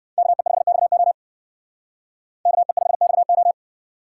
Morse code, heard as a keyed tone, about 56 wpm, 700 Hz. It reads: KE5XO KE5XO
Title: Speed Racing - USA 2x2 Call Signs 45wpm